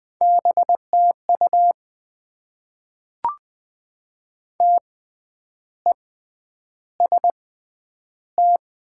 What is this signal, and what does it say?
Morse code, heard as a keyed tone, about 20 wpm, spaced as for 9 wpm, 700 Hz. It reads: BTU TEST
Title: Top 90 QSO Elements - 6x Character Spacing - 20wpm